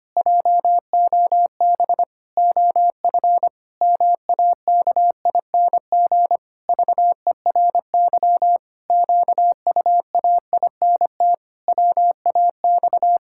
Morse code, heard as a keyed tone, about 25 wpm, 700 Hz. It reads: JOB OF MAKING VERY QUAINT WAX
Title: Pangram Challenge v2 - 25wpm